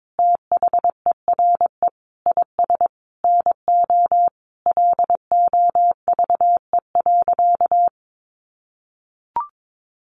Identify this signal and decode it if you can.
Morse code, heard as a keyed tone, about 22 wpm, 700 Hz. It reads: THERE IS NO LOVE.
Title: Sentences from Top 1200 Words 22wpm